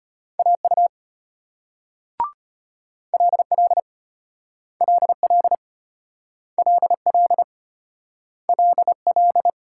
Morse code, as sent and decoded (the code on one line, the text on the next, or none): .- ..- / .-.. .-.. / .-.. .-.. / .-.. .-.. / .-.. .-..
AU LL LL LL LL